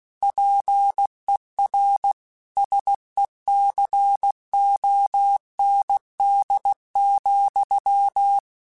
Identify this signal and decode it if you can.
Morse code, heard as a keyed tone, about 16 wpm, 800 Hz. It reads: PER SECOND,
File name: Technical Subjects / Technical Subjects-mp3s / Reactance @15